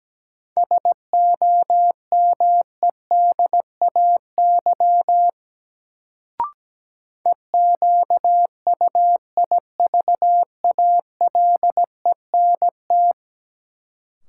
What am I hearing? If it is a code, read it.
Morse code, heard as a keyed tone, about 17 wpm, 700 Hz. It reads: SOMEDAY EQUIVALENT
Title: Sentences from Top 1600 Words - New Word Drill - 17wpm